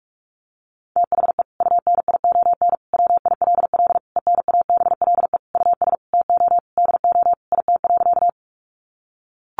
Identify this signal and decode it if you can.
Morse code, heard as a keyed tone, about 44 wpm, 700 Hz. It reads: THE UNION WILL ENABLE US TO DO IT.
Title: Sentences from Top 1500 Words 45wpm